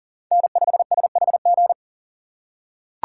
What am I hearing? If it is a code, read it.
Morse code, heard as a keyed tone, about 40 wpm, 700 Hz. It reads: N5SHG
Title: USA Call Signs - 5 Characters - 6x Character Spacing - 40wpm